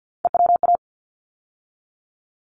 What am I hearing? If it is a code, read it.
Morse code, heard as a keyed tone, about 50 wpm, 700 Hz. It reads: EWA
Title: ARRL Sections 50wpm